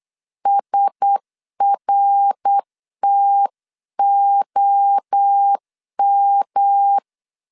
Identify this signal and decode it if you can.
Morse code, heard as a keyed tone, about 8 wpm, 800 Hz. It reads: SRTOM
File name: 8wpm